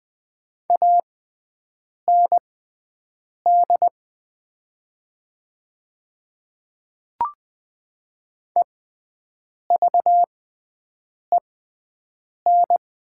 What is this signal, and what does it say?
Morse code, heard as a keyed tone, about 20 wpm, spaced as for 7 wpm, 700 Hz. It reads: AND EVEN
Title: 6x Character Spacing - Top 200 Words - 20wpm